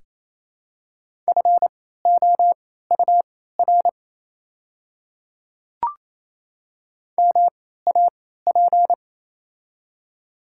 Morse code, heard as a keyed tone, about 28 wpm, spaced as for 16 wpm, 700 Hz. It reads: FOUR MAP
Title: 3x Character Spacing - Top 500 Words - 28wpm